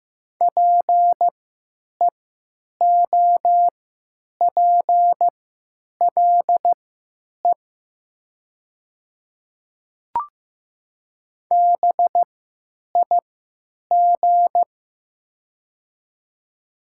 Morse code, heard as a keyed tone, about 15 wpm, spaced as for 9 wpm, 700 Hz. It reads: PEOPLE BIG
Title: 3x Character Spacing - Top 200 Words - 15wpm